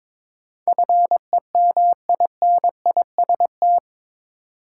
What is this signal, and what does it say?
Morse code, heard as a keyed tone, about 22 wpm, 700 Hz. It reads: FEMINIST
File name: practice / 22wpm / Eight Letter Words - 22wpm